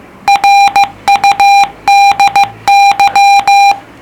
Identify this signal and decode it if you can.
Morse code, heard as a keyed tone, about 15 wpm, 800 Hz. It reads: RUDY